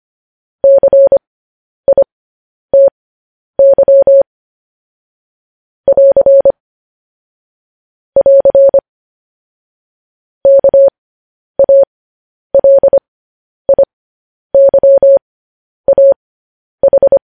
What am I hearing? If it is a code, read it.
Morse code, heard as a keyed tone, about 25 wpm, spaced as for 10 wpm, 550 Hz. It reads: CITY <AR> <AR> KALIYAH
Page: News Headlines in Morse Code at 10 WPM